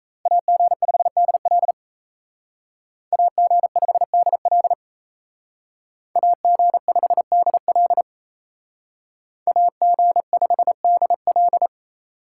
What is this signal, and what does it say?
Morse code, heard as a keyed tone, about 35 wpm, 700 Hz. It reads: AG5DL AG5DL AG5DL AG5DL
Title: Speed Racing - USA Call Signs 28wpm